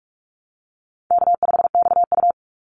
Morse code, heard as a keyed tone, about 44 wpm, 700 Hz. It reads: K5XU
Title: Easy Call Signs Repeated 3x - 45wpm